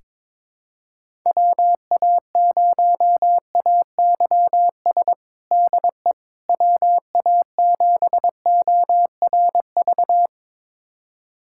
Morse code, heard as a keyed tone, about 22 wpm, 700 Hz. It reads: WA0AYS DE WA7ORV
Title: USA Callsign DE Callsign - 6 Characters - 22wpm